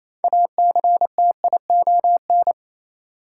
Morse code, heard as a keyed tone, about 28 wpm, 700 Hz. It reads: ACTION
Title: Sentences from Top 700 Words - New Word Drill - 28wpm